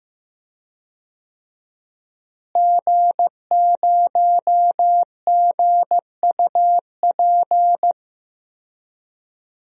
Morse code, heard as a keyed tone, about 15 wpm, 700 Hz. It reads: G0GUP
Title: DXCC-100 Call Signs - 5 Characters - Repeated 3x - 15wpm